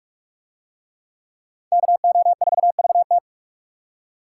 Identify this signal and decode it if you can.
Morse code, heard as a keyed tone, about 41 wpm, 700 Hz. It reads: KO4VT